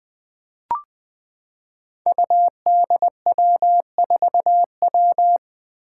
Morse code, heard as a keyed tone, about 20 wpm, 700 Hz. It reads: UDW4W
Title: Five Challenging Letter-Number Combinations 20wpm